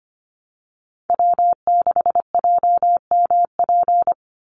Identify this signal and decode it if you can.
Morse code, heard as a keyed tone, about 25 wpm, 700 Hz. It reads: W6JMP
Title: NA SOTA Chaser Callsigns 25wpm